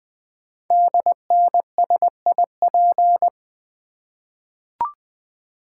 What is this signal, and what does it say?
Morse code, heard as a keyed tone, about 20 wpm, 700 Hz. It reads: DNSIP
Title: Code Groups - 5 Letters - 20wpm